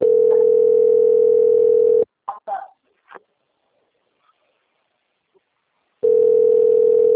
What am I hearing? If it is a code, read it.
Ringback tone (North America), 440 + 480 Hz.
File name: out- - - - - .2170